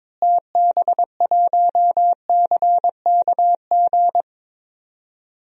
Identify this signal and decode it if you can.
Morse code, heard as a keyed tone, about 22 wpm, 700 Hz. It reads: TB1CKG
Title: DXCC-100 Call Signs - 6 Characters - Repeated 3x - 22wpm